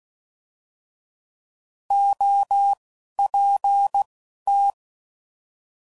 Morse code, heard as a keyed tone, about 16 wpm, spaced as for 12 wpm, 800 Hz. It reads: OPT
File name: Prefixes / By First Letter / Os